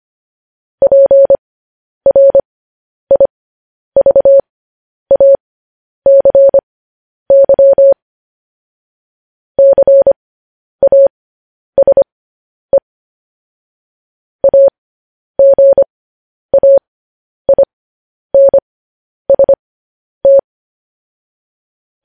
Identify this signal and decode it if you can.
Morse code, heard as a keyed tone, about 25 wpm, spaced as for 10 wpm, 550 Hz. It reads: PRIVACY CASE AGAINST
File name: podcast / News headlines-1x-25x10